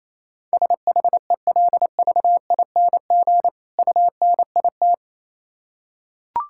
Morse code, heard as a keyed tone, about 27 wpm, 700 Hz. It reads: SHELVING UNIT
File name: Easy Words - Speed-Racing with Hints - 22wpm